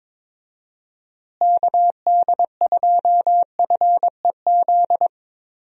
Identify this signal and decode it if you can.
Morse code, heard as a keyed tone, about 22 wpm, 700 Hz. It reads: KD2FEZ
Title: USA Call Signs - 6 Characters - Repeated 2x - 22wpm